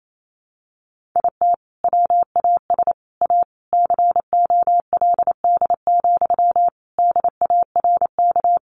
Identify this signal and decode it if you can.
Morse code, heard as a keyed tone, about 28 wpm, 700 Hz. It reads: IT WAS A COLD, DARK
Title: Sentences from Top 500 Words 28wpm